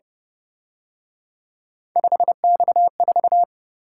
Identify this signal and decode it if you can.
Morse code, heard as a keyed tone, about 29 wpm, 700 Hz. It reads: 5X4
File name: Speed-Racing - Three Letter-Number Combinations - 25wpm